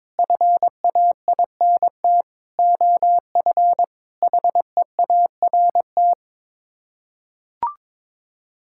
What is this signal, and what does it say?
Morse code, heard as a keyed tone, about 22 wpm, 700 Hz. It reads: FAINT OF HEART
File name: practice / 22wpm / Sets of 6 Words - 22wpm